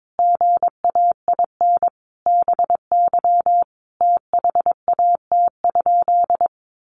Morse code, heard as a keyed tone, about 22 wpm, 700 Hz. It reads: GAIN BY THAT?
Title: Sentences from Top 1100 Words 22wpm